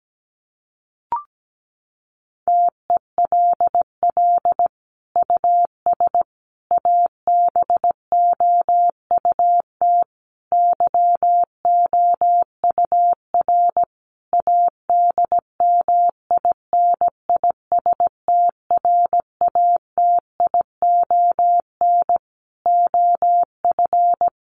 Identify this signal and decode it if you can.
Morse code, heard as a keyed tone, about 17 wpm, 700 Hz. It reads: TELL US ABOUT YOUR ADMINISTRATION OF
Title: Sentences from Top 1300 Words 17wpm